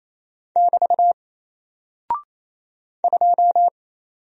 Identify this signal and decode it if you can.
Morse code, heard as a keyed tone, about 28 wpm, 700 Hz. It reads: <BT> 2
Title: Single Character (Letter:Number:Punctuation:Prosign) Alt 28wpm